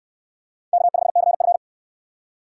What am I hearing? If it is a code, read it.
Morse code, heard as a keyed tone, about 58 wpm, 700 Hz. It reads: K5CF